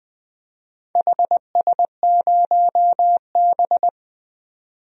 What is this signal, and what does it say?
Morse code, heard as a keyed tone, about 20 wpm, 700 Hz. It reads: HS0B
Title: DXCC-100 Call Signs - 4 Characters - 20wpm